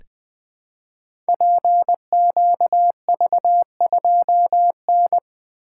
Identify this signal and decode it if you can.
Morse code, heard as a keyed tone, about 20 wpm, 700 Hz. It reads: PQV2N
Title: Code Groups - 5 Letter:Numbers - 20wpm